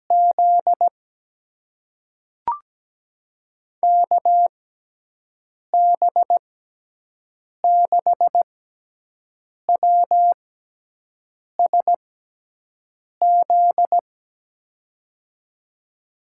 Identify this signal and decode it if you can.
Morse code, heard as a keyed tone, about 17 wpm, spaced as for 6 wpm, 700 Hz. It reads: Z KB6WSZ
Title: USA Call Signs - 6 Characters - 6x Character Spacing - 17wpm